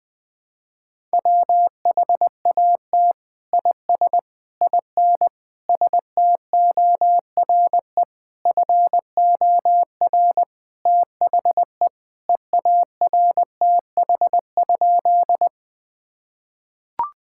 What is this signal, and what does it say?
Morse code, heard as a keyed tone, about 20 wpm, 700 Hz. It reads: WHAT IS IN STORE FOR THE EARTH?
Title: Sentences from Top 1000 Words - Review Part 5 - 20wpm